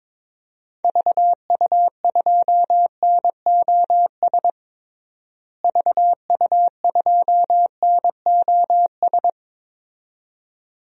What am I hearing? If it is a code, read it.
Morse code, heard as a keyed tone, about 22 wpm, 700 Hz. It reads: VU2NOS VU2NOS
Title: DXCC-100 Call Signs - 6 Characters - Repeated 2x - 22wpm